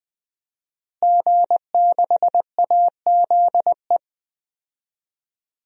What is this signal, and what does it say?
Morse code, heard as a keyed tone, about 20 wpm, 700 Hz. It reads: G6AZE
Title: DXCC-100 Call Signs - 5 Characters - Repeated 2x - 20wpm